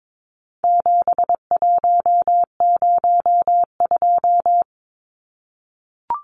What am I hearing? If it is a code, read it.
Morse code, heard as a keyed tone, about 22 wpm, 700 Hz. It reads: 7102
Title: Numbers - 4-Digit 22wpm